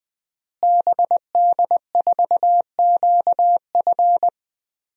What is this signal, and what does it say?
Morse code, heard as a keyed tone, about 20 wpm, 700 Hz. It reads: BD4QF
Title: DXCC-100 Call Signs - 5 Characters - Repeated 3x - 20wpm